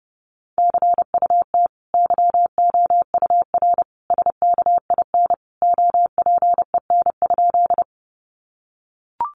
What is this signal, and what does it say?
Morse code, heard as a keyed tone, about 28 wpm, 700 Hz. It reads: CUT YOUR SKIN OPEN?
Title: Sentences from Top 1300 Words 30wpm